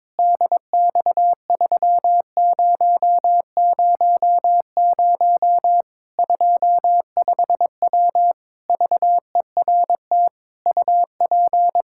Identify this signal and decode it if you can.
Morse code, heard as a keyed tone, about 22 wpm, 700 Hz. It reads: DX3000 25W VERT UP
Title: Practice QSO - Exchange 2 - Info - 22wpm